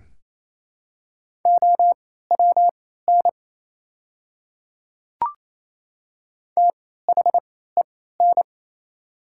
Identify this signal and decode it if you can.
Morse code, heard as a keyed tone, about 28 wpm, spaced as for 16 wpm, 700 Hz. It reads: OWN THEN
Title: 3x Character Spacing - Top 300 Words - 28wpm